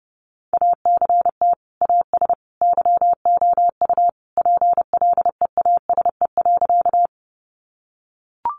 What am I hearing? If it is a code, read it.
Morse code, heard as a keyed tone, about 30 wpm, 700 Hz. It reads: ACT AS YOU PLEASE.
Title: Sentences from Top 1500 Words 30wpm